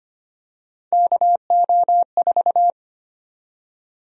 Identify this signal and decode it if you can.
Morse code, heard as a keyed tone, about 25 wpm, 700 Hz. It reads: KO4
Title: USA Call Sign Prefixes - 25wpm